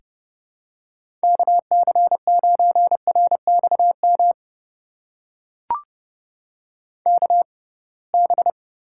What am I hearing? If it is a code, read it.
Morse code, heard as a keyed tone, about 30 wpm, spaced as for 23 wpm, 700 Hz. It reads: KC9RXM KB
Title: USA Call Signs - 6 Characters - 6x Character Spacing - 30wpm